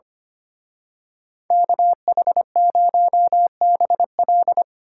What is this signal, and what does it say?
Morse code, heard as a keyed tone, about 29 wpm, 700 Hz. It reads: KH0BL
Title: Speed Racing - USA 2x2 Call Signs 25wpm